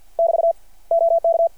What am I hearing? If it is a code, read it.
Morse code, heard as a keyed tone, about 53 wpm, 660 Hz. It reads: <BT> OK
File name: archived / CRSnet